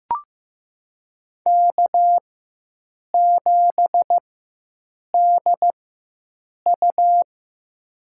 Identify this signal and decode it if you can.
Morse code, heard as a keyed tone, about 15 wpm, spaced as for 7 wpm, 700 Hz. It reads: K7DU
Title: USA Call Signs - 4x Character Spacing - 15wpm